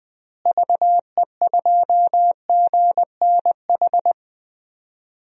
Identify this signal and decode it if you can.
Morse code, heard as a keyed tone, about 20 wpm, 700 Hz. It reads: VE2GNH
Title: DXCC-100 Call Signs - 6 Characters - Repeated 3x - 20wpm